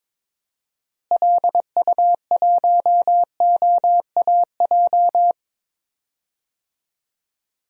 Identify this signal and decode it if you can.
Morse code, heard as a keyed tone, about 22 wpm, 700 Hz. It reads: LU1OAJ
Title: DXCC-100 Call Signs - 6 Characters - Repeated 2x - 22wpm